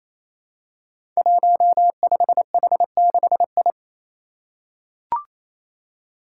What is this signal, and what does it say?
Morse code, heard as a keyed tone, about 29 wpm, 700 Hz. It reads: 15H6I
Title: Five Challenging Letter-Number Combinations 28wpm